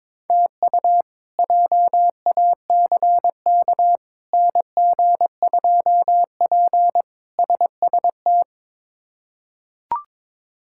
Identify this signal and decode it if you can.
Morse code, heard as a keyed tone, about 22 wpm, 700 Hz. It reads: TU JACK NG2P SST
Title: K1USN SST QSO Practice - 22wpm